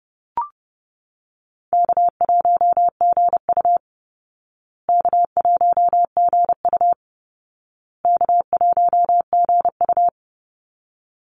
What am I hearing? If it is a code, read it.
Morse code, heard as a keyed tone, about 30 wpm, 700 Hz. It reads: K1GU K1GU K1GU
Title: Call Signs Repeated 2x - 30wpm